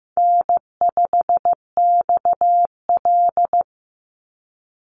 Morse code, heard as a keyed tone, about 15 wpm, 700 Hz. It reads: N5XL
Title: NA SOTA Chaser Callsigns 15wpm